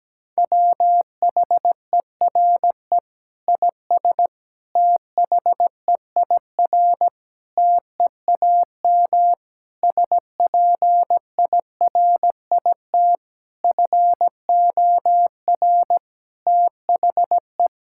Morse code, heard as a keyed tone, about 17 wpm, 700 Hz. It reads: WHERE IS THEIR TEAM SPIRIT FOR THE
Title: Sentences from Top 1600 Words 17wpm